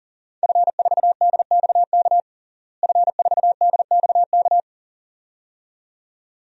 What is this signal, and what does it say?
Morse code, heard as a keyed tone, about 40 wpm, 700 Hz. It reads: F4DXK F4DXK
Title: DXCC-100 Call Signs - 5 Characters - Repeated 2x - 40wpm